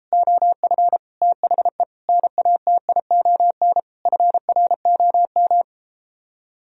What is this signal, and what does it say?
Morse code, heard as a keyed tone, about 33 wpm, 700 Hz. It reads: OF THE NATION FROM